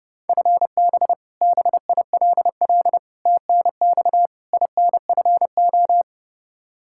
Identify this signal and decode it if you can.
Morse code, heard as a keyed tone, about 30 wpm, 700 Hz. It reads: FB BILL TNX INFO
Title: Practice QSO - Exchange 3 - More Info - 30wpm